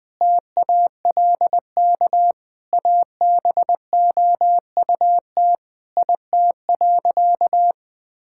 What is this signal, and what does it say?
Morse code, heard as a keyed tone, about 21 wpm, 700 Hz. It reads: TALK ABOUT IT.
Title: Sentences with 5-Word Phrases - 20wpm